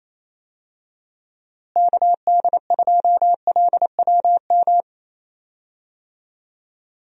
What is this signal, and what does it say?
Morse code, heard as a keyed tone, about 28 wpm, 700 Hz. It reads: KD2LWM
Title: USA Call Signs - 6 Characters - Repeated 3x - 28wpm